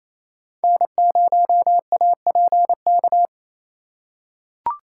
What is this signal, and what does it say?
Morse code, heard as a keyed tone, about 27 wpm, 700 Hz. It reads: N0APK
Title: USA Call Signs - 5 Characters - 4x Character Spacing - 28wpm